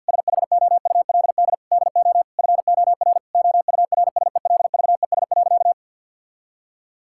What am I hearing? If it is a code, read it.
Morse code, heard as a keyed tone, about 50 wpm, 700 Hz. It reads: SHOULD DO FOR OURSELVES.
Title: Sentences from Top 1600 Words 50wpm